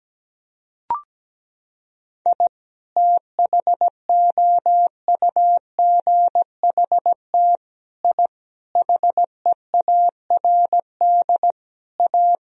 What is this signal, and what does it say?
Morse code, heard as a keyed tone, about 17 wpm, 700 Hz. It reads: I THOUGHT I HEARD A
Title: Sentences from Top 1000 Words - Review Part 2 - 17wpm